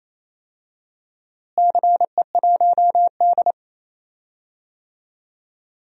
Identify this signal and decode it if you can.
Morse code, heard as a keyed tone, about 27 wpm, 700 Hz. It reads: CE1D
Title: DXCC-100 Call Signs - 4 Characters - Repeated 2x - 28wpm